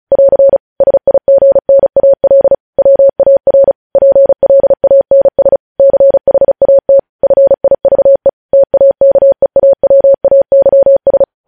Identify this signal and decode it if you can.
Morse code, heard as a keyed tone, about 36 wpm, 550 Hz. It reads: <AR> SIGNAL WAR PLANS CHAT FIVE TAKEAWAYS
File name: podcast / News headlines-1x-35x35